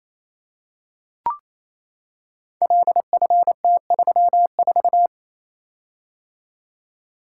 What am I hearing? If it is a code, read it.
Morse code, heard as a keyed tone, about 27 wpm, 700 Hz. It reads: LFT34
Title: Code Groups - 5 Letter:Numbers - 28wpm